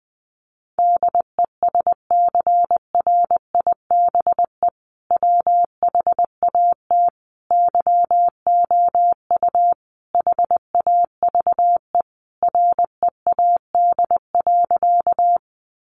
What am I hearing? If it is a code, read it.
Morse code, heard as a keyed tone, about 20 wpm, 700 Hz. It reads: DESCRIBE WHAT YOU HAVE READ.
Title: Sentences from Top 1000 Words 20wpm